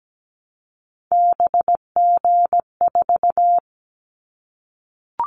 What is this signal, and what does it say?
Morse code, heard as a keyed tone, about 17 wpm, 700 Hz. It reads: BG4
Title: Callsign Prefixes for Top 100 DXCC - 17wpm